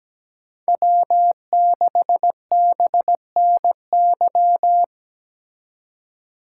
Morse code, heard as a keyed tone, about 20 wpm, 700 Hz. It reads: W6BNY
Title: Speed Racing - USA Call Signs 17wpm